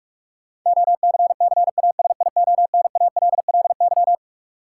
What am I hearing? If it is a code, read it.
Morse code, heard as a keyed tone, about 46 wpm, 700 Hz. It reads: OCCASIONALLY
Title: Sentences from Top 1300 Words - New Word Drill Alt - 45wpm